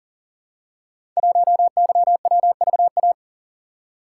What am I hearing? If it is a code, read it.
Morse code, heard as a keyed tone, about 38 wpm, 700 Hz. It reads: 1YWVA
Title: Code Groups - 5 Letter:Numbers - 40wpm